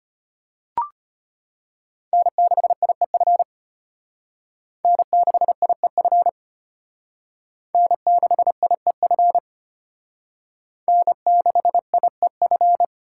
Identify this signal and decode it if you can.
Morse code, heard as a keyed tone, about 32 wpm, 700 Hz. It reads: N6IEF N6IEF N6IEF N6IEF